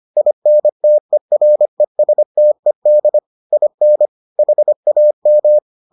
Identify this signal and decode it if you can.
Morse code, heard as a keyed tone, about 25 wpm, 600 Hz. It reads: INTERESTED IN HAM